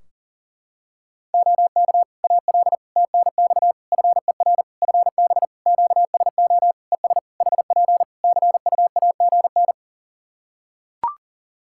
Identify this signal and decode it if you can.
Morse code, heard as a keyed tone, about 40 wpm, 700 Hz. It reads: OK AL TNX FER FB QSO ES HP CUAGN
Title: Practice QSO - Exchange 4 - Ending - 40wpm